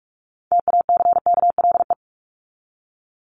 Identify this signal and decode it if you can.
Morse code, heard as a keyed tone, about 44 wpm, 700 Hz. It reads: TACKLE
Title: Top 1751-2000 Words 45wpm